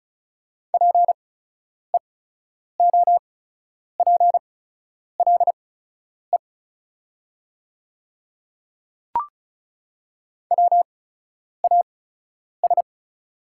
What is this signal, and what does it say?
Morse code, heard as a keyed tone, about 34 wpm, spaced as for 10 wpm, 700 Hz. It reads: PEOPLE WAS
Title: 8x Character Spacing - Top 100 Words - 35wpm